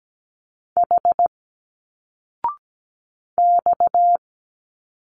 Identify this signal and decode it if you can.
Morse code, heard as a keyed tone, about 17 wpm, 700 Hz. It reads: H X
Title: Single Letters 17wpm